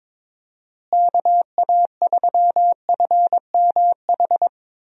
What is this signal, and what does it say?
Morse code, heard as a keyed tone, about 22 wpm, 700 Hz. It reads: KA3FMH
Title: USA Call Signs - 6 Characters - Repeated 2x - 22wpm